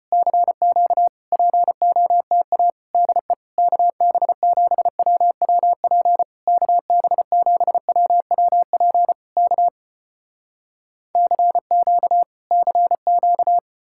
Morse code, heard as a keyed tone, about 35 wpm, 700 Hz. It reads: CQ POTA DE KB7WWP KB7WWP K CQ CQ